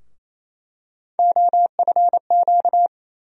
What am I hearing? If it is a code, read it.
Morse code, heard as a keyed tone, about 27 wpm, 700 Hz. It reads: OFQ
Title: USA Call Sign Postfixes - 3 Letters 28wpm